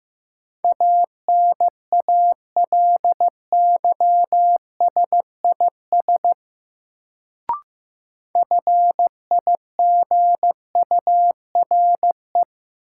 Morse code, heard as a keyed tone, about 15 wpm, 700 Hz. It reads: ANALYSIS FIGURE
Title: Sentences from Top 1400 Words - New Word Drill Alt - 15wpm